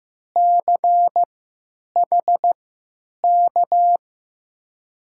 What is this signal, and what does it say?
Morse code, heard as a keyed tone, about 15 wpm, spaced as for 9 wpm, 700 Hz. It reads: CHK
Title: USA Call Signs - 3x Character Spacing - 15wpm